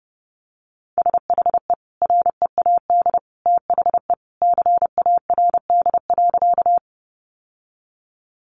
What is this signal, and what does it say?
Morse code, heard as a keyed tone, about 30 wpm, 700 Hz. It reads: SHE READ THE CARD.
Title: Sentences from Top 600 Words 30wpm